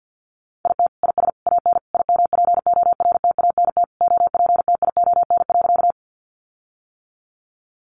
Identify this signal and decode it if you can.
Morse code, heard as a keyed tone, about 50 wpm, 700 Hz. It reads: IT IS AN IMPORTANT OPTION.
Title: Sentences from Top 1300 Words 50wpm